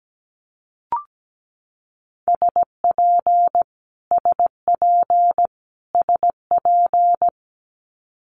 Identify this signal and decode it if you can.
Morse code, heard as a keyed tone, about 17 wpm, 700 Hz. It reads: SP SP SP
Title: Multi-letter Phonemes Repeated 3x 17wpm